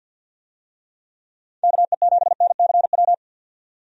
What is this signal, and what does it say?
Morse code, heard as a keyed tone, about 49 wpm, 700 Hz. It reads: KE7NCW
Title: USA Call Signs - 6 Characters - 50wpm